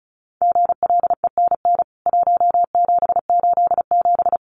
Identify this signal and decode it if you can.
Morse code, heard as a keyed tone, about 34 wpm, 700 Hz. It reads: GLENN 1787
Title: CWT Exchange 35wpm